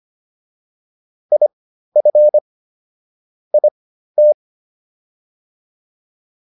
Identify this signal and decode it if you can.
Morse code, heard as a keyed tone, about 25 wpm, spaced as for 13 wpm, 600 Hz. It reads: IF IT